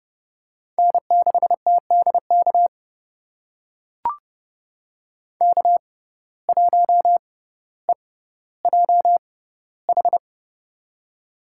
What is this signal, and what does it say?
Morse code, heard as a keyed tone, about 30 wpm, spaced as for 15 wpm, 700 Hz. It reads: N6TDK K1EJH
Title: USA Call Signs - 5 Characters - 6x Character Spacing - 30wpm